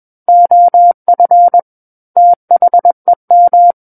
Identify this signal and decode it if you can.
Morse code, heard as a keyed tone, about 21 wpm, 700 Hz. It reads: OF THEM
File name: Book One / 12-Chapter 12 - 21 WPM